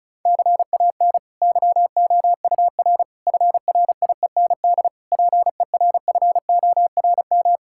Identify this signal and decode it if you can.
Morse code, heard as a keyed tone, about 34 wpm, 690 Hz. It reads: CAN YOUR FRIEND PERFORM
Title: Sentences from Top 1000 Words - Review Part 5 - 35wpm